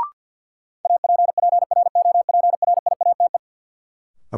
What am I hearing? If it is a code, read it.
Morse code, heard as a keyed tone, about 48 wpm, 700 Hz. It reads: APPROPRIATE
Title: Sentences from Top 900 Words - New Word Drill - 50wpm